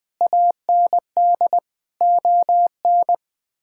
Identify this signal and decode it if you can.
Morse code, heard as a keyed tone, about 21 wpm, 700 Hz. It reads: AND ON
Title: Sets of 8 Words - 20wpm